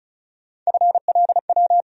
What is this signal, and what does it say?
Morse code, heard as a keyed tone, about 34 wpm, 700 Hz. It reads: FLW
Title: USA Call Sign Postfixes - 3 Letters 35wpm